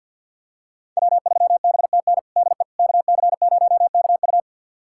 Morse code, heard as a keyed tone, about 52 wpm, 690 Hz. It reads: W3BTN DE KC0KU
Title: USA Callsign DE Callsign - 5 Characters - 50wpm